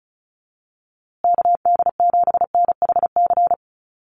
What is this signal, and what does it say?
Morse code, heard as a keyed tone, about 34 wpm, 700 Hz. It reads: KD7NHC
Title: USA Call Signs Repeated 3x 35wpm